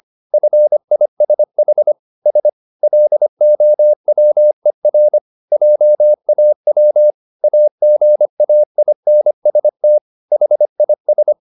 Morse code, heard as a keyed tone, about 25 wpm, 600 Hz. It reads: FISH S LOWER JAW AGAINST HIS